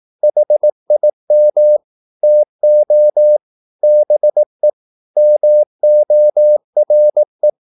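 Morse code, heard as a keyed tone, about 18 wpm, 600 Hz. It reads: HIM TO BE MORE
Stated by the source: Morse practice at 18 wpm, English text, from CW med Gnister.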